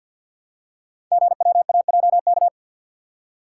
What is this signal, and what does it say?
Morse code, heard as a keyed tone, about 52 wpm, 700 Hz. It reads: GWAJK
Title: Code Groups - 5 Letter:Numbers - 50wpm